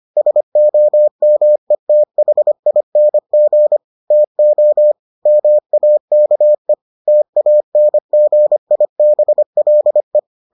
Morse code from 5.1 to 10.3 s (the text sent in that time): MAKE TANGIBLE